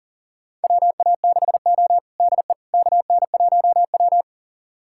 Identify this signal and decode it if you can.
Morse code, heard as a keyed tone, about 37 wpm, 700 Hz. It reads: WA6O DE KN1W
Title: USA Callsign DE Callsign - 4 Characters - 40wpm